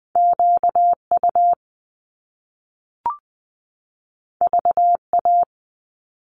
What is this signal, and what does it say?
Morse code, heard as a keyed tone, about 20 wpm, 700 Hz. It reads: QU VA
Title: Top Two Letter Combinations in Words 20wpm